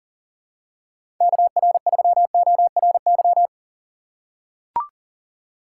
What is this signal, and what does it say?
Morse code, heard as a keyed tone, about 40 wpm, 700 Hz. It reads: KR3ORY
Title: USA Call Signs - 4x Character Spacing - 40wpm